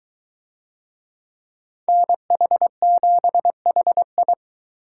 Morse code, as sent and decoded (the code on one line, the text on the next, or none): -. .... --... .... ..
NH7HI